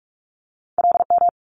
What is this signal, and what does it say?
Morse code, heard as a keyed tone, about 46 wpm, 700 Hz. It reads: LM
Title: Multi-letter Phonemes Quick Fire 45wpm